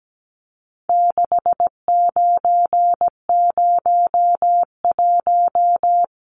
Morse code, heard as a keyed tone, about 17 wpm, 700 Hz. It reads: 6901
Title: Numbers - 4-Digit 17wpm